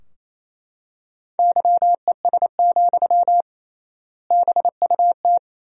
Morse code, heard as a keyed tone, about 28 wpm, 700 Hz. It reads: YES, BUT
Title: 3x Word-Spacing - Sentences from Top 500 Words - 28wpm